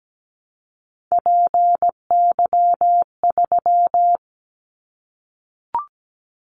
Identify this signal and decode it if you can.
Morse code, heard as a keyed tone, about 17 wpm, 700 Hz. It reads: PY3
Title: Callsign Prefixes for Top 100 DXCC - 17wpm